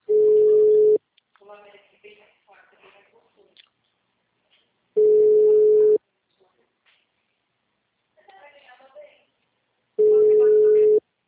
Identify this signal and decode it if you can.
Ringback tone (Europe, 425 Hz), with speech.
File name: out- -2001- - - .40025